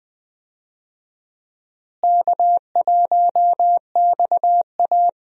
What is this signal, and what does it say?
Morse code, heard as a keyed tone, about 20 wpm, 700 Hz. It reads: K1XA
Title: USA Call Signs - 4 Characters - Repeated 2x - 20wpm